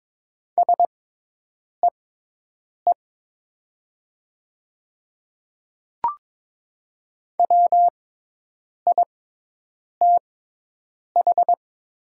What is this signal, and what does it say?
Morse code, heard as a keyed tone, about 22 wpm, spaced as for 8 wpm, 700 Hz. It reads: SEE WITH
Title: 6x Character Spacing - Top 200 Words - 22wpm